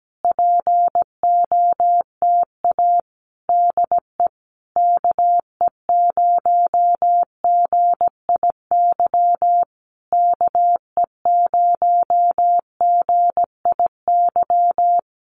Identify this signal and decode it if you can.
Morse code, heard as a keyed tone, about 17 wpm, 700 Hz. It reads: POTA DE KE0GIY KE0GIY
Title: Common QSO Phrases Encore 17wpm